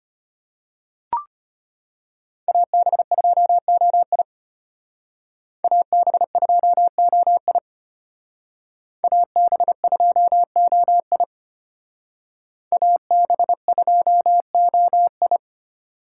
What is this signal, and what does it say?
Morse code, heard as a keyed tone, about 32 wpm, 700 Hz. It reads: AB2OI AB2OI AB2OI AB2OI